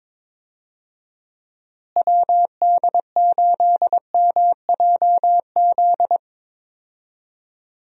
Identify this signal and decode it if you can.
Morse code, heard as a keyed tone, about 22 wpm, 700 Hz. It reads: WD8MJZ
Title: USA Call Signs - 6 Characters - Repeated 3x - 22wpm